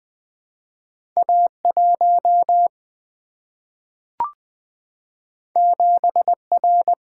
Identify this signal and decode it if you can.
Morse code, heard as a keyed tone, about 20 wpm, 700 Hz. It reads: A 1 7 R
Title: Single-Letter Single-Number 20wpm